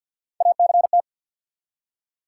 Morse code, heard as a keyed tone, about 47 wpm, 700 Hz. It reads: ACT
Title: Three Letter Words - 50wpm